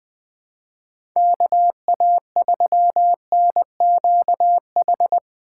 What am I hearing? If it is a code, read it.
Morse code, heard as a keyed tone, about 21 wpm, 700 Hz. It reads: KA3NQH
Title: USA Call Signs - 6 Characters - 4x Character Spacing - 20wpm